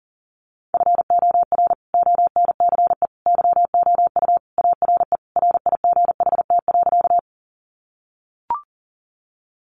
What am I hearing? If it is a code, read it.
Morse code, heard as a keyed tone, about 40 wpm, 700 Hz. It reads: FOR ONCE YOU ARE RIGHT.
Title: Sentences from Top 300 Words 40wpm